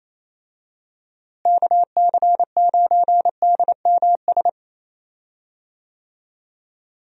Morse code, heard as a keyed tone, about 27 wpm, 700 Hz. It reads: KC9DMS
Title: USA Call Signs - 6 Characters - Repeated 3x - 28wpm